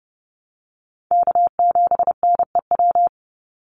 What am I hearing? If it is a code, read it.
Morse code, heard as a keyed tone, about 30 wpm, 700 Hz. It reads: K7NEW
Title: NA SOTA Chaser Callsigns 30wpm